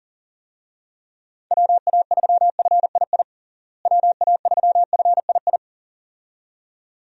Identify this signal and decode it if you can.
Morse code, heard as a keyed tone, about 40 wpm, 700 Hz. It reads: WA3FII WA3FII
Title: USA Call Signs - 6 Characters - Repeated 2x - 40wpm